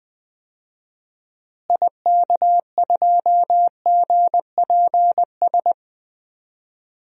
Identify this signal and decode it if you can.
Morse code, heard as a keyed tone, about 20 wpm, 700 Hz. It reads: IK2GPS
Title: DXCC-100 Call Signs - 6 Characters - Repeated 2x - 20wpm